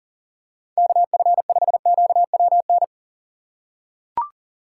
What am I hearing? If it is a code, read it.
Morse code, heard as a keyed tone, about 40 wpm, 700 Hz. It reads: KF5QWN
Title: USA Call Signs - 6 Characters - 4x Character Spacing - 40wpm